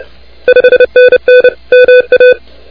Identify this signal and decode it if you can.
Morse code, heard as a keyed tone, about 29 wpm, 2500 Hz, with noise. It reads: 5NN MA
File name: recordings / K5NA de NT1K ARRL10M CW 28MHz